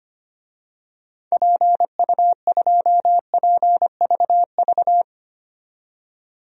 Morse code, heard as a keyed tone, about 25 wpm, 700 Hz. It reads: PU2PVV
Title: DXCC-100 Call Signs - 6 Characters - 25wpm